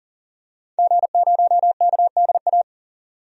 Morse code, heard as a keyed tone, about 40 wpm, 700 Hz. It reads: G0KDA
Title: DXCC-100 Call Signs - 5 Characters - 40wpm